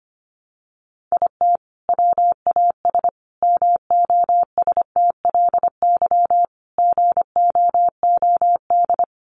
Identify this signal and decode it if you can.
Morse code, heard as a keyed tone, about 25 wpm, 700 Hz. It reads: IT WAS MOSTLY GOOD
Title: Sentences from Top 1300 Words 25wpm